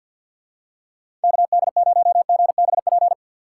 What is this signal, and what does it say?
Morse code, heard as a keyed tone, about 49 wpm, 700 Hz. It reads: KD0GBP
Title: USA Call Signs - 6 Characters - 6x Character Spacing - 50wpm